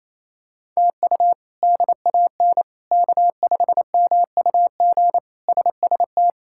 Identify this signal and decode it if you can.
Morse code, heard as a keyed tone, about 28 wpm, 700 Hz. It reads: TU DAN K5MUG SST
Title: K1USN SST QSO Practice - 28wpm